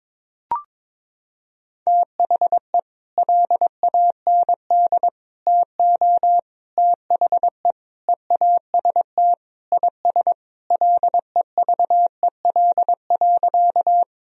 Morse code, heard as a keyed tone, about 22 wpm, 700 Hz. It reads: THE LAND TO THE EAST IS LEVEL.
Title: Sentences from Top 1000 Words - Review Part 4 - 22wpm